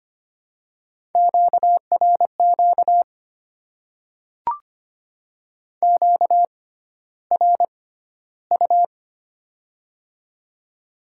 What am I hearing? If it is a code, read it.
Morse code, heard as a keyed tone, about 25 wpm, spaced as for 13 wpm, 700 Hz. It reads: QRQ QRU
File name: Top 90 QSO Elements - 6x Character Spacing - 25wpm